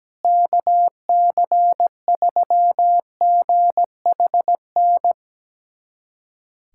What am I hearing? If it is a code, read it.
Morse code, heard as a keyed tone, about 17 wpm, 700 Hz. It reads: KC3GHN
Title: USA Callsign DE Callsign - 6 Characters - 17wpm